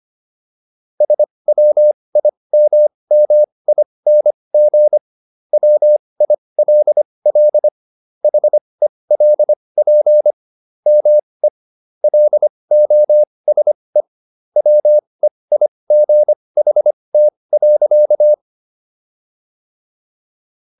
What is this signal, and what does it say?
Morse code, heard as a keyed tone, about 25 wpm, spaced as for 20 wpm, 600 Hz. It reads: SWIMMING WILL HELP ME LOSE WEIGHT.